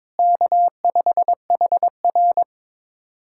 Morse code, heard as a keyed tone, about 22 wpm, 700 Hz. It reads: K5HR
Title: USA Callsign DE Callsign - 4 Characters - 22wpm